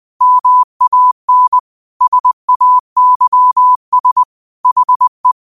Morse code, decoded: MAN SAYS HE

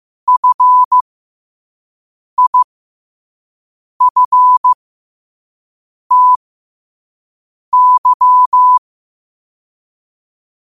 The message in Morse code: FIFTY

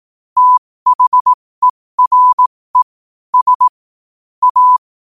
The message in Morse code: THERE S A

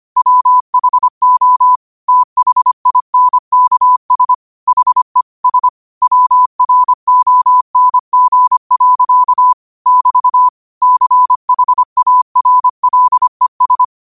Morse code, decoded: WHO THINKS HE S WRONG. <BT> CHARLES

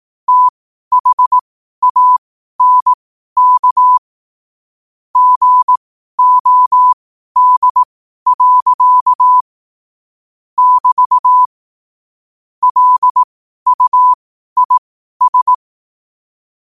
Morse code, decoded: THANK GOD. <BT> LUIS

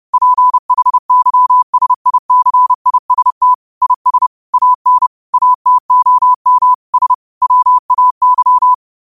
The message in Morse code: PHYSICIST IS AN ATOM S WAY

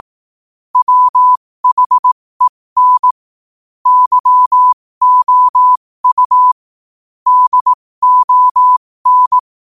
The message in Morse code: WHEN YOU DON

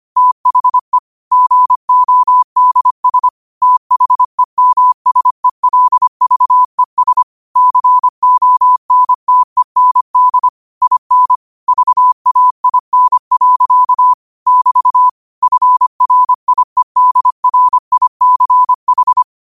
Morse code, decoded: THE GODS THEMSELVES CONTEND IN VAIN. <BT> FRIEDRICH

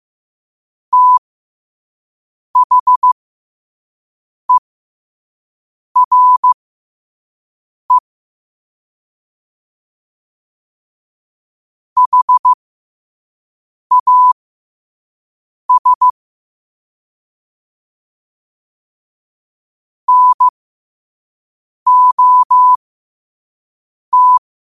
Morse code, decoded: THERE HAS NOT

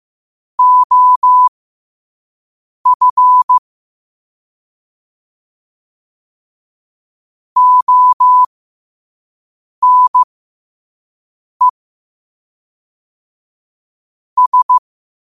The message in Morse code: OF ONE S

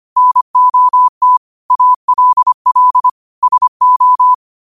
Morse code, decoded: NOT ALL SO

